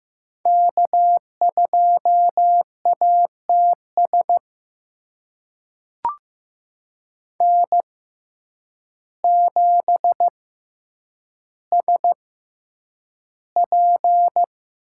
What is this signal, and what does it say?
Morse code, heard as a keyed tone, about 15 wpm, spaced as for 8 wpm, 700 Hz. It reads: K2ATS N7SP
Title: USA Call Signs - 5 Characters - 6x Character Spacing - 15wpm